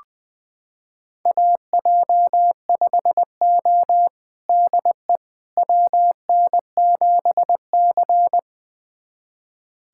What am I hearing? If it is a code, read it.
Morse code, heard as a keyed tone, about 20 wpm, 700 Hz. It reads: AJ5O DE WN7C
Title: USA Callsign DE Callsign - 4 Characters - 20wpm